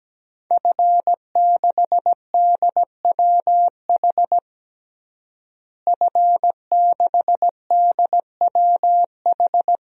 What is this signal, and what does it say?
Morse code, heard as a keyed tone, about 17 wpm, 700 Hz. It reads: F6DWH F6DWH
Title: DXCC-100 Call Signs - 5 Characters - Repeated 2x - 17wpm